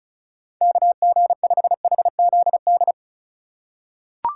Morse code, heard as a keyed tone, about 34 wpm, 700 Hz. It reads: KG5HZD
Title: USA Call Signs - 4x Character Spacing - 35wpm